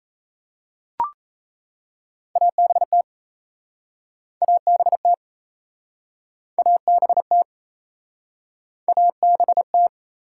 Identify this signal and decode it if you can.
Morse code, heard as a keyed tone, about 37 wpm, 700 Hz. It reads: ABT ABT ABT ABT